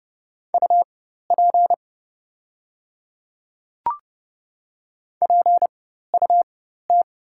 Morse code, heard as a keyed tone, about 29 wpm, spaced as for 14 wpm, 700 Hz. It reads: UP PUT
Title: 4x Character Spacing - Top 300 Words - 30wpm